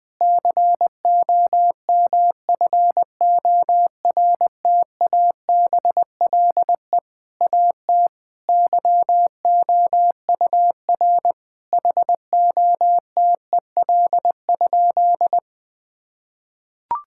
Morse code, heard as a keyed tone, about 20 wpm, 700 Hz. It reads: COMFORTABLE AT YOUR HOTEL?
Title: Sentences from Top 1600 Words 20wpm